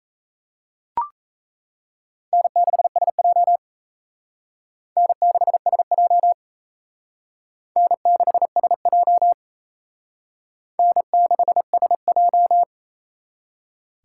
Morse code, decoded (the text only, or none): N6SJ N6SJ N6SJ N6SJ